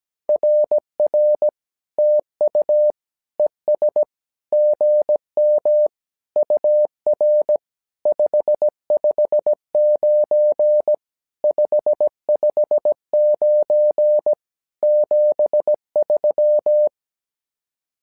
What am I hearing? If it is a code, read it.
Morse code, heard as a keyed tone, about 17 wpm, 600 Hz. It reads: RR TU ES GM UR 559 559 73
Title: SOTA QSO Practice 17wpm